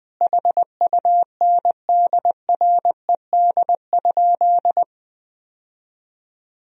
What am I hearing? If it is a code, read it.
Morse code, heard as a keyed tone, about 21 wpm, 700 Hz. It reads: HUNDRED?
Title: Sentences from Top 1000 Words - Review Part 2 - 20wpm